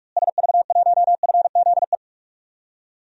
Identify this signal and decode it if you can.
Morse code, heard as a keyed tone, about 46 wpm, 700 Hz. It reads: SV1FZE
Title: DXCC-100 Call Signs - 6 Characters - Repeated 3x - 45wpm